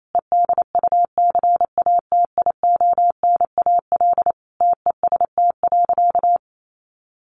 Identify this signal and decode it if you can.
Morse code, heard as a keyed tone, about 28 wpm, 700 Hz. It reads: EDUCATIONAL TEST.
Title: Sentences from Top 1200 Words 28wpm